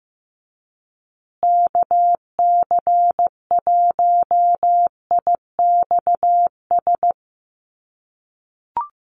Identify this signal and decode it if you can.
Morse code, heard as a keyed tone, about 15 wpm, 700 Hz. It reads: KC1IXS
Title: USA Call Signs Repeated 3x 15wpm